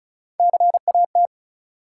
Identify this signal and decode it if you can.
Morse code, heard as a keyed tone, about 36 wpm, 700 Hz. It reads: CAT
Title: Three Letter Words - 35wpm